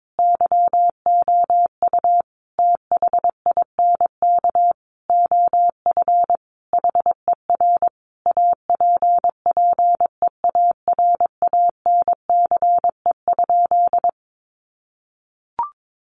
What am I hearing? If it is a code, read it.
Morse code, heard as a keyed tone, about 22 wpm, 700 Hz. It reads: YOU THINK OF HER APPEARANCE?
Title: Sentences from Top 1100 Words 22wpm